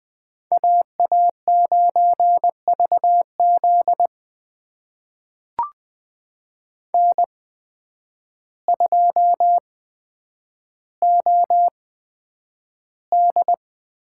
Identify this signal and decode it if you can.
Morse code, heard as a keyed tone, about 20 wpm, spaced as for 9 wpm, 700 Hz. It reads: AA9VZ N2OD
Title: USA Call Signs - 8x Character Spacing - 20wpm